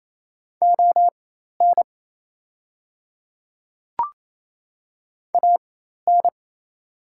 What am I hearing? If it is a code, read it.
Morse code, heard as a keyed tone, about 28 wpm, spaced as for 13 wpm, 700 Hz. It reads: ON AN
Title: 4x Character Spacing - Top 100 Words - 28wpm